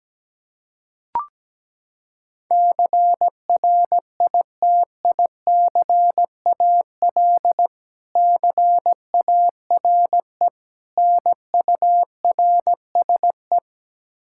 Morse code, heard as a keyed tone, about 17 wpm, 700 Hz. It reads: CRITICAL CARE NURSE
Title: Practice QSO - Exchange 3 - More Info - 17wpm